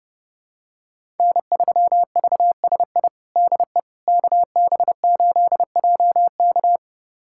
Morse code, heard as a keyed tone, about 32 wpm, 700 Hz. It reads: N3VSI DE KB8JK
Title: USA Callsign DE Callsign - 5 Characters - 30wpm